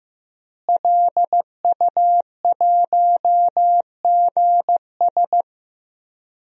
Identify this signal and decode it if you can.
Morse code, heard as a keyed tone, about 16 wpm, 700 Hz. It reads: LU1GS
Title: DXCC-100 Call Signs - 5 Characters - Repeated 3x - 15wpm